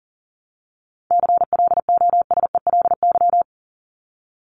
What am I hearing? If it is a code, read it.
Morse code, heard as a keyed tone, about 40 wpm, 700 Hz. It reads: CLOSELY
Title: Top 1251-1500 Words 40wpm